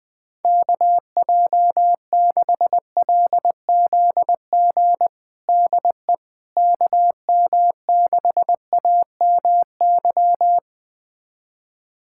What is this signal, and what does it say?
Morse code, heard as a keyed tone, about 20 wpm, 700 Hz. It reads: KJ6LZG DE KM6AMY
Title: USA Callsign DE Callsign - 6 Characters - 20wpm